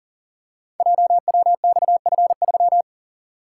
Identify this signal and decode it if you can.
Morse code, heard as a keyed tone, about 41 wpm, 690 Hz. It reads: JWXF3
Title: Code Groups - 5 Letter:Numbers - 40wpm